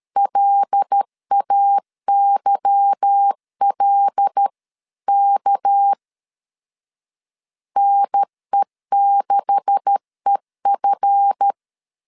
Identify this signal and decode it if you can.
Morse code, heard as a keyed tone, about 13 wpm, 800 Hz. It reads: LAYLK NE6EF
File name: CW / DET / 1 9 1500C S B 12wpm 3819